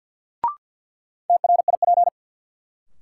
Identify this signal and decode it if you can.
Morse code, heard as a keyed tone, about 53 wpm, 700 Hz. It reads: TRIP
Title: Sentences from Top 1300 Words - New Word Drill Alt - 50wpm